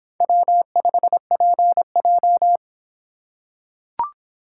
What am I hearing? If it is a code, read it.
Morse code, heard as a keyed tone, about 24 wpm, 700 Hz. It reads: W5PJ